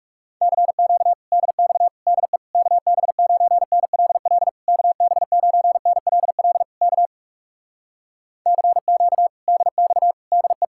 Morse code, heard as a keyed tone, about 40 wpm, 700 Hz. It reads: CQ DX DE KB9NLL KB9NLL K CQ DX DE